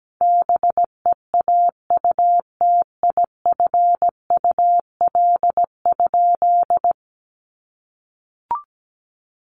Morse code, heard as a keyed tone, about 17 wpm, 700 Hz. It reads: BEAUTIFUL?
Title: Sentences from Top 1200 Words 17wpm